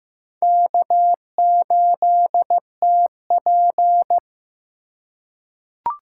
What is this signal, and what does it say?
Morse code, heard as a keyed tone, about 15 wpm, 700 Hz. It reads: K8TP
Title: USA Call Signs - 4 Characters - 3x Character Spacing - 15wpm